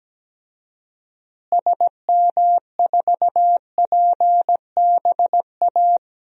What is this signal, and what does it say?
Morse code, heard as a keyed tone, about 17 wpm, 700 Hz. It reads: SM4PBA
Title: DXCC-100 Call Signs - 6 Characters - Repeated 3x - 17wpm